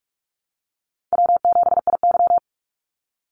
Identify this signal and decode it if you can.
Morse code, heard as a keyed tone, about 44 wpm, 700 Hz. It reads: W7IY
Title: Easy Call Signs Repeated 2x - 45wpm